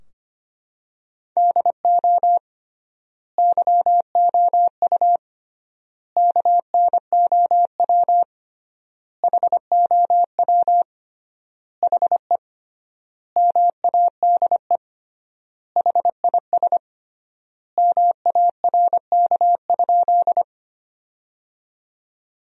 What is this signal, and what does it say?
Morse code, heard as a keyed tone, about 25 wpm, 700 Hz. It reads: DO YOU KNOW HOW HE MADE HIS MARK?
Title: 3x Word-Spacing - Sentences from Top 300 Words - 25wpm